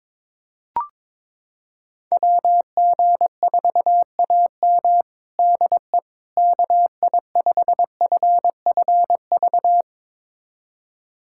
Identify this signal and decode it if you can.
Morse code, heard as a keyed tone, about 22 wpm, 700 Hz. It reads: WG4AM DE KI5FFV
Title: USA Callsign DE Callsign 22wpm